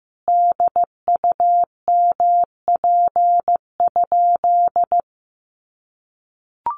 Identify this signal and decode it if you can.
Morse code, heard as a keyed tone, about 15 wpm, 700 Hz. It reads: DUMP?
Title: Sentences from Top 1300 Words 15wpm